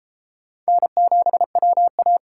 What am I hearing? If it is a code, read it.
Morse code, heard as a keyed tone, about 30 wpm, 700 Hz. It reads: N7WA